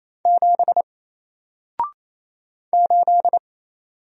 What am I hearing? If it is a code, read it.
Morse code, heard as a keyed tone, about 27 wpm, 700 Hz. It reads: 7 8
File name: Single-Letter-Number-28wpm